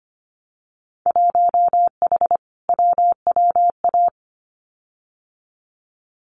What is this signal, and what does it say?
Morse code, heard as a keyed tone, about 25 wpm, 700 Hz. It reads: 1H WWA
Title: Winter Field Day Exchanges 25wpm